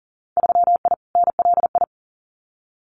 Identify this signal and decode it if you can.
Morse code, heard as a keyed tone, about 42 wpm, 700 Hz. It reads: 3I NLI
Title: Winter Field Day Exchanges 40wpm